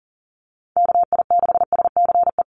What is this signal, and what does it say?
Morse code, heard as a keyed tone, about 40 wpm, 700 Hz. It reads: KI6SCE
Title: USA Call Signs Repeated 2x 40wpm